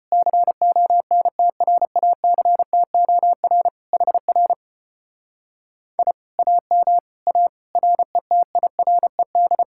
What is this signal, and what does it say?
Morse code, heard as a keyed tone, about 29 wpm, 700 Hz. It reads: CONTRACTOR HR I AM A RETIRED
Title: Speed-Racing Common QSO Phrases - 25wpm